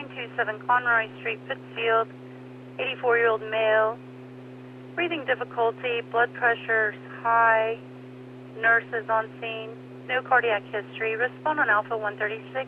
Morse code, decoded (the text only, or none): none